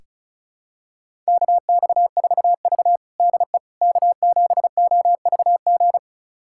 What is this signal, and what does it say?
Morse code, heard as a keyed tone, about 36 wpm, 700 Hz. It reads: KX4V DE K7OVG
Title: USA Callsign DE Callsign 35wpm